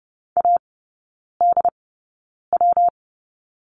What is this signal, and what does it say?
Morse code, heard as a keyed tone, about 30 wpm, spaced as for 9 wpm, 700 Hz. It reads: ADW
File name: Lesson 19e - Letter w - Three Characters 30wpm